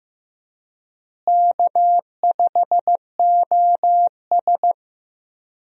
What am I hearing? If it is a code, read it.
Morse code, heard as a keyed tone, about 15 wpm, 700 Hz. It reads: K5OS
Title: USA Call Signs - 4 Characters - Repeated 3x - 15wpm